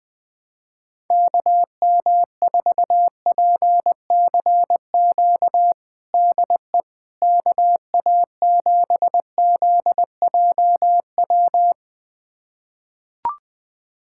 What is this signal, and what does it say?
Morse code, heard as a keyed tone, about 20 wpm, 700 Hz. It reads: KM4PCQ DE KA7ZJW
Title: USA Callsign DE Callsign - 6 Characters - 20wpm